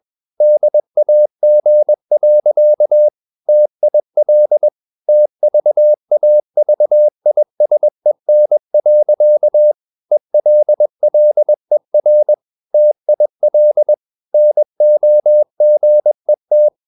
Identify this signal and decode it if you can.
Morse code, heard as a keyed tone, about 21 wpm, 600 Hz. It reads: DAG. TIL TVAVISEN. ELLER TIL NOGET
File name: Ferie QRS 21wpm